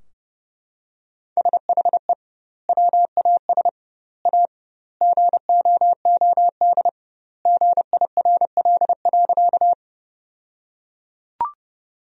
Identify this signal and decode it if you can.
Morse code, heard as a keyed tone, about 30 wpm, 700 Hz. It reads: SHE WAS A GOOD GIRL.
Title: 2x Word-Spacing - Sentences from Top 400 Words - 30wpm